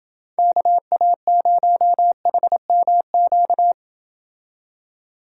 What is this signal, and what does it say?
Morse code, heard as a keyed tone, about 28 wpm, 700 Hz. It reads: KA0HMQ